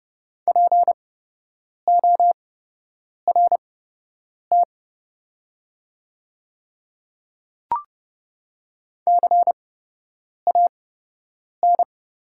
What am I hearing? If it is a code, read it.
Morse code, heard as a keyed tone, about 30 wpm, spaced as for 8 wpm, 700 Hz. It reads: PORT CAN
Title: 8x Character Spacing - Top 300 Words - 30wpm